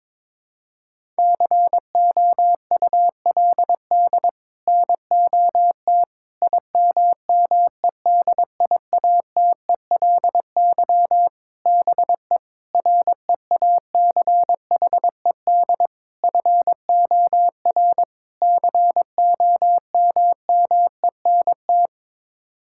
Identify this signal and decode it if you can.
Morse code, heard as a keyed tone, about 22 wpm, 700 Hz. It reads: COULD NOT IMMEDIATELY BE REACHED FOR COMMENT
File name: practice / 22wpm / Sets of 7 Words - 22wpm